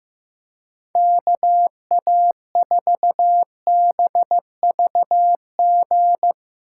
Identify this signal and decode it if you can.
Morse code, heard as a keyed tone, about 15 wpm, 700 Hz. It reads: KA4BVG
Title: K1USN SST QSO Practice - 15wpm